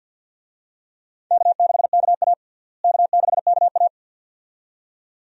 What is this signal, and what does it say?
Morse code, heard as a keyed tone, about 49 wpm, 700 Hz. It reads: K6KA K6KA
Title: USA Call Signs - 4 Characters - Repeated 2x - 50wpm